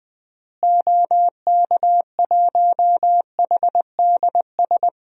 Morse code, heard as a keyed tone, about 20 wpm, 700 Hz. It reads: OK1HDS
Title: DXCC-100 Call Signs - 6 Characters - Repeated 3x - 20wpm